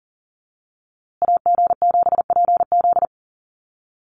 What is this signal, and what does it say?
Morse code, heard as a keyed tone, about 40 wpm, 700 Hz. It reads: AG7PZ
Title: USA Call Signs Repeated 2x 40wpm